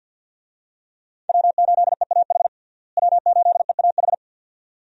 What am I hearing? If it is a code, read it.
Morse code, heard as a keyed tone, about 49 wpm, 700 Hz. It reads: W8EAH W8EAH
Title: USA Call Signs - 5 Characters - Repeated 2x - 50wpm